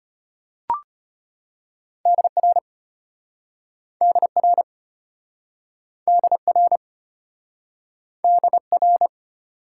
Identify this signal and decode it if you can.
Morse code, heard as a keyed tone, about 33 wpm, 700 Hz. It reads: DR DR DR DR